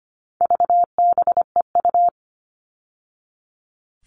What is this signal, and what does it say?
Morse code, heard as a keyed tone, about 25 wpm, 700 Hz. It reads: VBEU
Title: All Cut Numbers - 4-Digit 25wpm